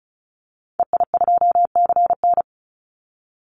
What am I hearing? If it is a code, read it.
Morse code, heard as a keyed tone, about 36 wpm, 700 Hz. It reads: EI2CN
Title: Call Signs - 35wpm